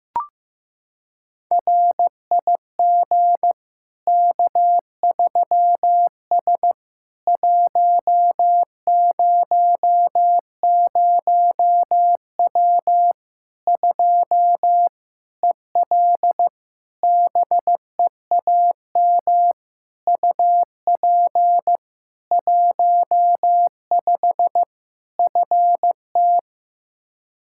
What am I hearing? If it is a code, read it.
Morse code, heard as a keyed tone, about 15 wpm, 700 Hz. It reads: RIG K3S 100W 2 EL BEAM UP 15 FT
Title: Practice QSO - Exchange 2 - Info - 15wpm